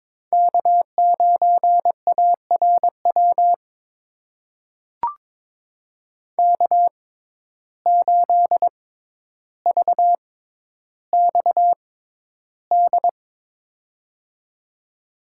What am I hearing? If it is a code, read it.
Morse code, heard as a keyed tone, about 21 wpm, spaced as for 11 wpm, 700 Hz. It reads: K9ARW K8VXD
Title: USA Call Signs - 5 Characters - 6x Character Spacing - 22wpm